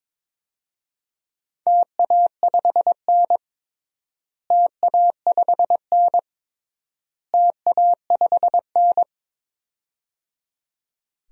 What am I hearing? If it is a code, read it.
Morse code, heard as a keyed tone, about 22 wpm, 700 Hz. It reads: TA5N TA5N TA5N
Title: DXCC-100 Call Signs - 4 Characters - Repeated 3x - 22wpm